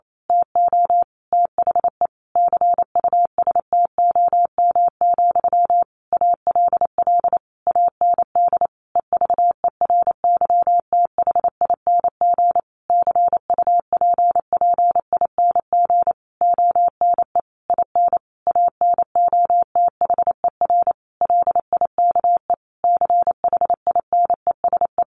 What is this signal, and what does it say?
Morse code, heard as a keyed tone, about 28 wpm, 700 Hz. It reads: TO THE CUSTOM, ALL AND EVERYTHING CUPPING ONE IN ANOTHER LIKE CHINESE
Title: A Sound Of Thunder 28wpm